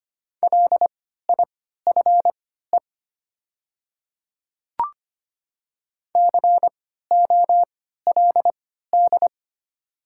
Morse code, heard as a keyed tone, about 25 wpm, spaced as for 14 wpm, 700 Hz. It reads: LIFE COLD
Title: 3x Character Spacing - Top 500 Words - 25wpm